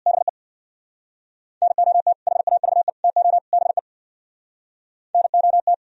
Morse code, heard as a keyed tone, about 55 wpm, 700 Hz. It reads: BE NOT HAVE TO BE NOT
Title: Speed-Racing - Sets of 4 Words 50wpm